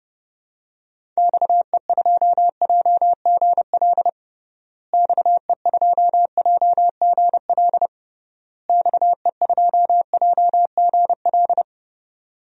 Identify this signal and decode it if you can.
Morse code, heard as a keyed tone, about 30 wpm, 700 Hz. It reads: XE2JGL XE2JGL XE2JGL
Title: DXCC-100 Call Signs - 6 Characters - Repeated 3x - 30wpm